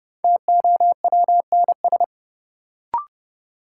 Morse code, heard as a keyed tone, about 30 wpm, 700 Hz. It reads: TOWNS
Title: Sentences from Top 1000 Words - Review Part 6 - New Word Drill - 30wpm